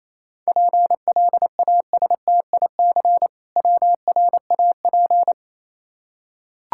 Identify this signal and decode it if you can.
Morse code, heard as a keyed tone, about 28 wpm, 700 Hz. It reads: PLASTIC WRAP